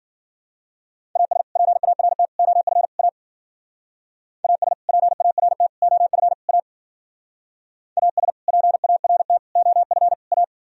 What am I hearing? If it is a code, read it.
Morse code, heard as a keyed tone, about 50 wpm, 700 Hz. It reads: AS PART OF A AS PART OF A AS PART OF A
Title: Speed-Racing - Sets of 4 Words 40wpm